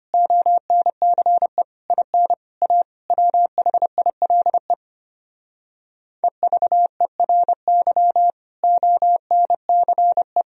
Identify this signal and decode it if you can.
Morse code, heard as a keyed tone, about 33 wpm, 700 Hz. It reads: ONCE IN A WHILE EVERY ONCE